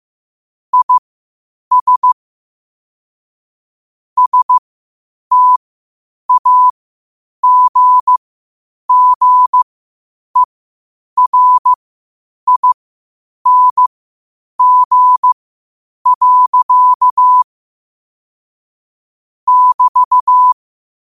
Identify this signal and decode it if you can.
Morse code, heard as a keyed tone, about 15 wpm, spaced as for 8 wpm, 1000 Hz. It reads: IS STAGGERING. <BT>
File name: qotdfiles / 8wpm